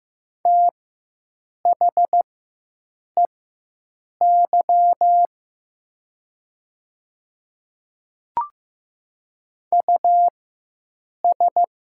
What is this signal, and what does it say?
Morse code, heard as a keyed tone, about 15 wpm, spaced as for 7 wpm, 690 Hz. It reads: THEY US
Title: 4x Character Spacing - Top 500 Words - 15wpm